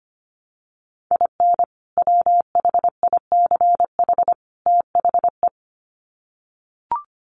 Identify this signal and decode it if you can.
Morse code, heard as a keyed tone, about 25 wpm, 700 Hz. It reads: IN WHICH THE
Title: Sets of 3 Words 25wpm